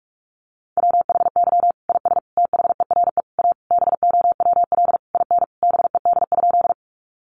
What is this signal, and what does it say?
Morse code, heard as a keyed tone, about 44 wpm, 700 Hz. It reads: WHY IS THERE A BOWL IN BED?
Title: Sentences from Top 1200 Words 45wpm